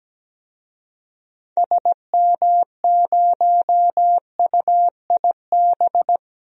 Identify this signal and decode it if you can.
Morse code, heard as a keyed tone, about 17 wpm, 700 Hz. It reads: SM0UIB
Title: DXCC-100 Call Signs - 6 Characters - Repeated 3x - 17wpm